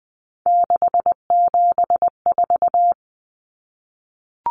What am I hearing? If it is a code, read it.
Morse code, heard as a keyed tone, about 20 wpm, 700 Hz. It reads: 674
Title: Common Cut Numbers - 3-Digit 20wpm